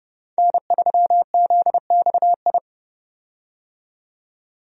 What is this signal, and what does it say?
Morse code, heard as a keyed tone, about 30 wpm, 700 Hz. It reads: N3ZXI
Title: USA Call Signs - 5 Characters - 30wpm